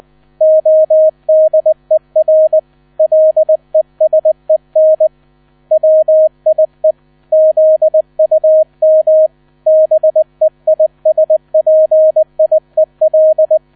Morse code, heard as a keyed tone, about 20 wpm, 630 Hz. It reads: ODER LESEN WIE ZUM BEISPIEL
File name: 000 Vorwort 20wpm